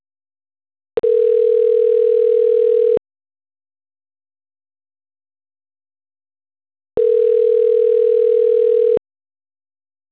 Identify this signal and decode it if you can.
Ringback tone (North America), 440 + 480 Hz.
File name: autotestagent / 11-23-57.136..callcentertransfer. .8aafaf16-e03e-4f4c-89f6